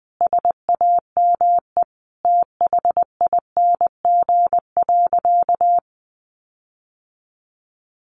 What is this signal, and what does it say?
Morse code, heard as a keyed tone, about 20 wpm, 700 Hz. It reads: SAME THING.
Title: Sentences from Top 1300 Words 20wpm